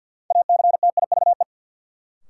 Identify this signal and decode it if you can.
Morse code, heard as a keyed tone, about 51 wpm, 700 Hz. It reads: ACTIVE
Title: Sentences from Top 900 Words - New Word Drill - 50wpm